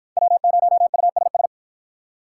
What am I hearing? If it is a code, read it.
Morse code, heard as a keyed tone, about 48 wpm, 700 Hz. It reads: W0USS